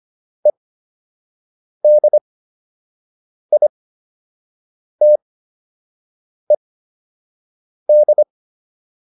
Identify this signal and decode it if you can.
Morse code, heard as a keyed tone, about 25 wpm, spaced as for 6 wpm, 600 Hz. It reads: EDITED